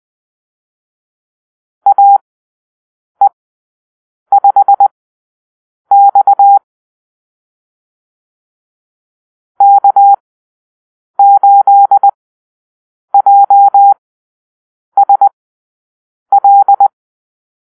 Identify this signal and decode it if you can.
Morse code, heard as a keyed tone, about 20 wpm, spaced as for 7 wpm, 800 Hz. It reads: AE5X K8JSL
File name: arrl / Call-Sign Drill-7